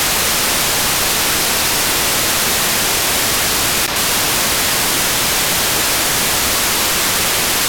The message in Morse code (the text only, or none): none